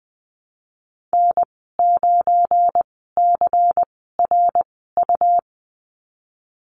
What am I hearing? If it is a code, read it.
Morse code, heard as a keyed tone, about 19 wpm, spaced as for 14 wpm, 700 Hz. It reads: N9CRU
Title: 186 Lesson 30g - Letter b - Call Signs 20wpm